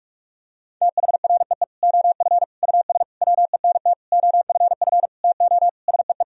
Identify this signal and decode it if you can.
Morse code, heard as a keyed tone, about 46 wpm, 700 Hz. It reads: THREE OF US WENT OFF TO SEE
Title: Sentences from Top 1000 Words - Review Part 1 - 45wpm